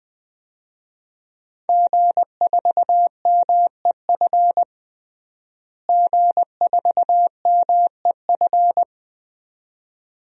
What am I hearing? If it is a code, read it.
Morse code, heard as a keyed tone, about 20 wpm, 700 Hz. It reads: G4MEF G4MEF
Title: DXCC-100 Call Signs - 5 Characters - Repeated 2x - 20wpm